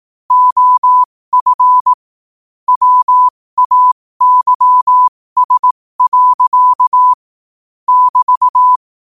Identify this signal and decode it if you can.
Morse code, heard as a keyed tone, about 18 wpm, spaced as for 15 wpm, 1000 Hz. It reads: OF WAYS. <BT>
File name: qotdfiles / 15wpm